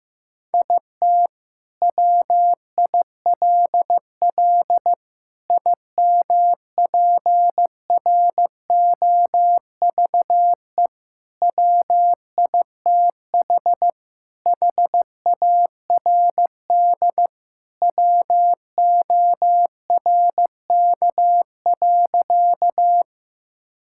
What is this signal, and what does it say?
Morse code, heard as a keyed tone, about 15 wpm, 700 Hz. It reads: IT WILL IMPROVE WITH HARD WORK.
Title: Sentences from Top 1000 Words - Review Part 5 - 15wpm